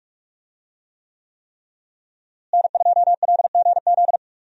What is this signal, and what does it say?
Morse code, heard as a keyed tone, about 46 wpm, 700 Hz. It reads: N2LGZ
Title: USA Call Signs - 5 Characters - Repeated 3x - 45wpm